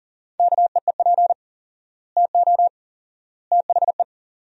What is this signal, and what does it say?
Morse code, heard as a keyed tone, about 40 wpm, 700 Hz. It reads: KEEP TO THE
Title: 4x Word-Spacing - Sentences from Top 300 Words - 40wpm